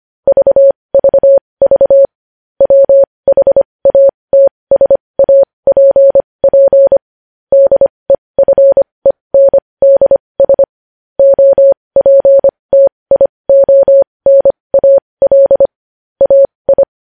Morse code, heard as a keyed tone, about 25 wpm, spaced as for 20 wpm, 550 Hz. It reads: VVV WHATSAPP DEFENDS OPTIONAL AI
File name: podcast / News headlines-1x-25x20